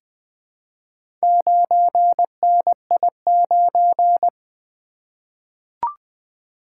Morse code, heard as a keyed tone, about 20 wpm, 700 Hz. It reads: 9NI9
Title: Four Letter-Number Combinations - 20wpm